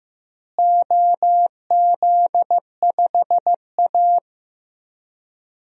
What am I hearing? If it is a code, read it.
Morse code, heard as a keyed tone, about 15 wpm, 700 Hz. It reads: OZ5A
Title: DXCC-100 Call Signs - 4 Characters - Repeated 3x - 15wpm